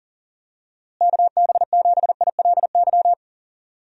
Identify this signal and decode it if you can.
Morse code, heard as a keyed tone, about 40 wpm, 700 Hz. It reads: KB7ILY
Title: USA Call Signs - 6 Characters - Repeated 3x - 40wpm